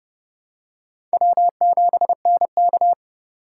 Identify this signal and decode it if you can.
Morse code, heard as a keyed tone, about 31 wpm, 700 Hz. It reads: W7NK